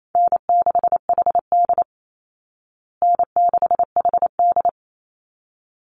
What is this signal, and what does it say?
Morse code, heard as a keyed tone, about 28 wpm, 700 Hz. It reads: N6HD N6HD
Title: Easy Call Signs Repeated 2x - 28wpm